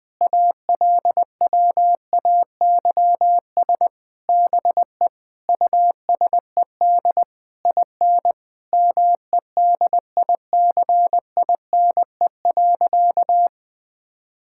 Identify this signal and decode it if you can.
Morse code, heard as a keyed tone, about 21 wpm, 700 Hz. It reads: ALWAYS BE USED IN MEDICINE.
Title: Sentences from Top 1000 Words - Review Part 6 - 20wpm